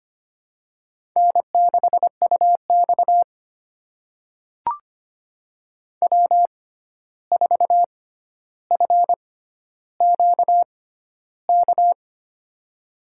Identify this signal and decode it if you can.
Morse code, heard as a keyed tone, about 25 wpm, spaced as for 12 wpm, 700 Hz. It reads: N6UX W4FQK
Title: USA Call Signs - 6x Character Spacing - 25wpm